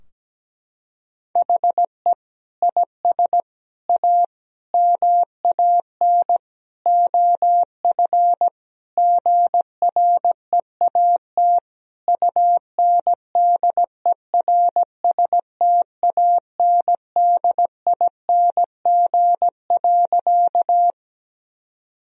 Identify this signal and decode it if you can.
Morse code, heard as a keyed tone, about 17 wpm, 700 Hz. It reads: HE IS A MAN OF GREAT UNDERSTANDING.
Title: Sentences from Top 1000 Words - Review Part 3 - 17wpm